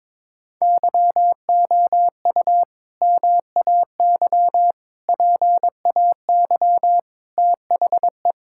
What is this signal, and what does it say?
Morse code, heard as a keyed tone, about 22 wpm, 700 Hz. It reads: YOU MAY PAY THE
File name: practice / 22wpm / Sentences with 8-Word Phrases - 22wpm